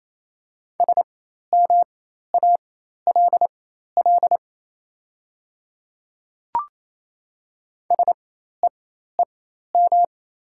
Morse code, heard as a keyed tone, about 27 wpm, spaced as for 13 wpm, 700 Hz. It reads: SMALL SEEM
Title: 4x Character Spacing - Top 300 Words - 28wpm